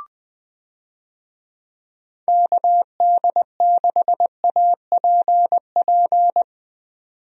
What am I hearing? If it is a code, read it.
Morse code, heard as a keyed tone, about 21 wpm, 700 Hz. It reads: KD6APP
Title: USA Call Signs - 6 Characters - 20wpm